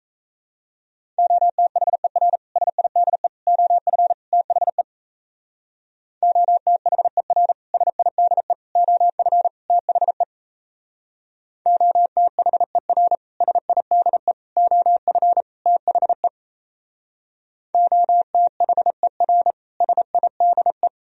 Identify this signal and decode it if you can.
Morse code, heard as a keyed tone, about 37 wpm, 700 Hz. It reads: OTHER SIDE OF THE OTHER SIDE OF THE OTHER SIDE OF THE OTHER SIDE